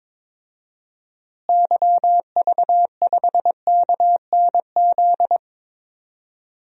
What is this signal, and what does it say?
Morse code, heard as a keyed tone, about 21 wpm, 700 Hz. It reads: YV5KNZ
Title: DXCC-100 Call Signs - 6 Characters - Repeated 3x - 22wpm